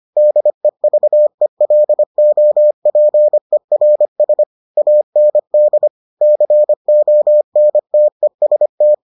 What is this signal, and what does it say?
Morse code, heard as a keyed tone, about 25 wpm, 600 Hz. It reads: DEVELOPERS AND CONTEST